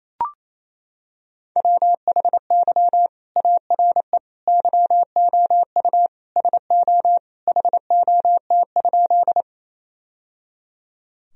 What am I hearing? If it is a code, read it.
Morse code, heard as a keyed tone, about 28 wpm, 700 Hz. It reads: WHY ARE YOU SO HOT?
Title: Sentences from Top 1000 Words - Review Part 3 - 28wpm